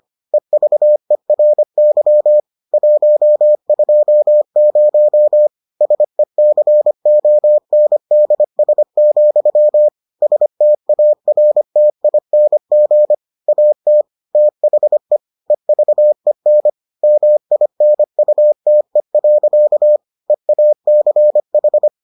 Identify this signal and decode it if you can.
Morse code, heard as a keyed tone, about 25 wpm, 600 Hz. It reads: EVERY 120 SECONDS, STARTING AT THE EVEN MINUTE. EACH